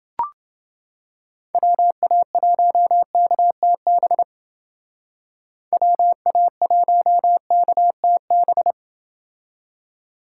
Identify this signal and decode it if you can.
Morse code, heard as a keyed tone, about 28 wpm, 700 Hz. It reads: WA1KTB WA1KTB